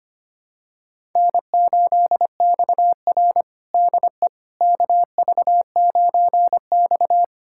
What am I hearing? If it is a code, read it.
Morse code, heard as a keyed tone, about 25 wpm, 700 Hz. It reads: N8XR DE KV9X
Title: USA Callsign DE Callsign - 4 Characters - 25wpm